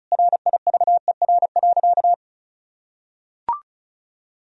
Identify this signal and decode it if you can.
Morse code, heard as a keyed tone, about 34 wpm, 700 Hz. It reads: RIVER.
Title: Sentences from Top 1000 Words - Review Part 1 - 35wpm